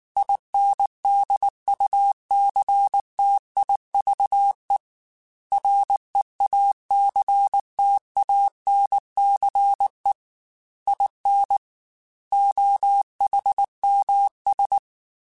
Morse code, heard as a keyed tone, about 19 wpm, 800 Hz. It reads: INDUCTIVE REACTANCE IN OHMS